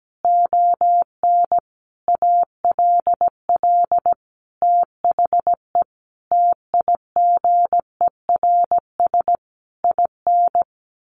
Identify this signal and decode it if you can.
Morse code, heard as a keyed tone, about 17 wpm, 700 Hz. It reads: ON ALL THE TIGERS IN
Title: A Sound Of Thunder 17wpm